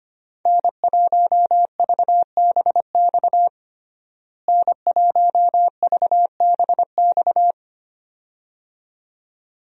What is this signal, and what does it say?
Morse code, heard as a keyed tone, about 25 wpm, 700 Hz. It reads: N1VBX N1VBX
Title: USA Call Signs - 5 Characters - Repeated 2x - 25wpm